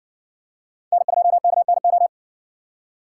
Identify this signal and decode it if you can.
Morse code, heard as a keyed tone, about 56 wpm, 700 Hz. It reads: N2KNO